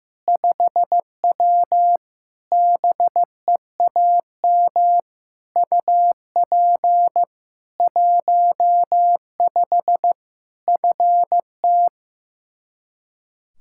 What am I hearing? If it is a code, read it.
Morse code, heard as a keyed tone, about 15 wpm, 700 Hz. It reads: 5W BEAM UP 15 FT
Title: K1USN SST QSO Practice 15wpm